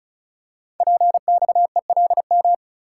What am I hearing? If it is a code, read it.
Morse code, heard as a keyed tone, about 36 wpm, 690 Hz. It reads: PXELM
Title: Code Groups - 5 Letters - 35wpm